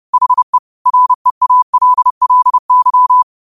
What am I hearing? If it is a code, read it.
Morse code, heard as a keyed tone, about 30 wpm, 1000 Hz. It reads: HE REALLY